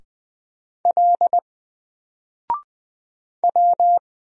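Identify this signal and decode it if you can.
Morse code, heard as a keyed tone, about 20 wpm, 700 Hz. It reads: L W
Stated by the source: Single letter and number Morse drill at 20 wpm, from N4LCD.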